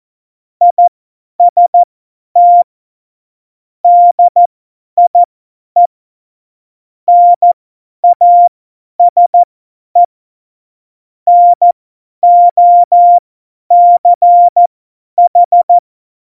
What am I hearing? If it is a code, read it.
Morse code, heard as a keyed tone, about 14 wpm, spaced as for 10 wpm, 700 Hz. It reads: IST DIE NASE NOCH
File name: Morsekurs / Witze / Witz 17. Oktober 17, Tempo 50